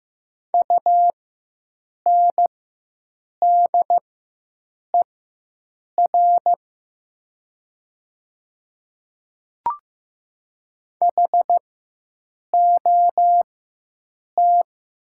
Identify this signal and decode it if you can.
Morse code, heard as a keyed tone, about 15 wpm, spaced as for 7 wpm, 700 Hz. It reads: UNDER HOT
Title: 4x Character Spacing - Top 500 Words - 15wpm